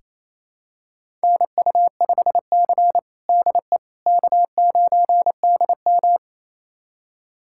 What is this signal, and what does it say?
Morse code, heard as a keyed tone, about 27 wpm, 690 Hz. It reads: NU5C DE K9DM
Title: USA Callsign DE Callsign - 4 Characters - 28wpm